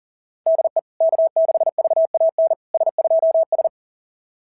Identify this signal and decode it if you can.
Morse code, heard as a keyed tone, about 44 wpm, 660 Hz. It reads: DE K6VAN S2S
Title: SOTA QSO Practice 40wpm